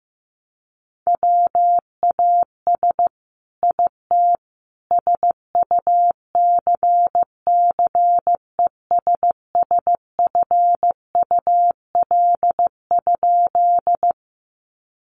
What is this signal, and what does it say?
Morse code, heard as a keyed tone, about 15 wpm, 700 Hz. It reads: WAS IT SUCCESSFUL?
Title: A Sound Of Thunder 15wpm